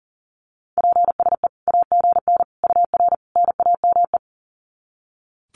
Morse code, heard as a keyed tone, about 40 wpm, 700 Hz. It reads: PSE AGN UR NAME
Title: Common QSO Phrases 40wpm